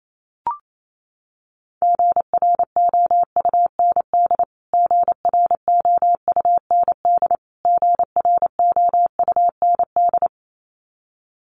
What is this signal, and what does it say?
Morse code, heard as a keyed tone, about 28 wpm, 700 Hz. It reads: GROUND GROUND GROUND
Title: Top 500 Words Repeated 28wpm